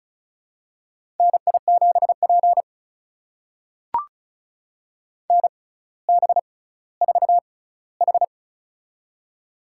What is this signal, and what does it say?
Morse code, heard as a keyed tone, about 34 wpm, spaced as for 18 wpm, 700 Hz. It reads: NI7P NB4H
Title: USA Call Signs - 4 Characters - 6x Character Spacing - 35wpm